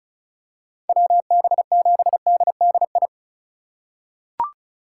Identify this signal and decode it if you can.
Morse code, heard as a keyed tone, about 36 wpm, 700 Hz. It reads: WB7DDI
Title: USA Call Signs - 2x Character Spacing - 35wpm